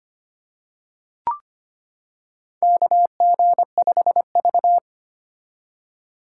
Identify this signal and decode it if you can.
Morse code, heard as a keyed tone, about 25 wpm, 700 Hz. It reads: KG5V
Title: K1USN SST QSO Practice - 25wpm